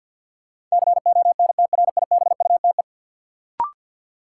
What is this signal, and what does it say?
Morse code, heard as a keyed tone, about 47 wpm, 700 Hz. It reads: CONTRIBUTE
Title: Sentences from Top 1400 Words - New Word Drill Alt - 50wpm